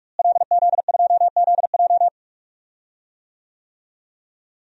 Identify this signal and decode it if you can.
Morse code, heard as a keyed tone, about 46 wpm, 700 Hz. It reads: LZ2ZJ
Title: DXCC-100 Call Signs - 5 Characters - Repeated 2x - 45wpm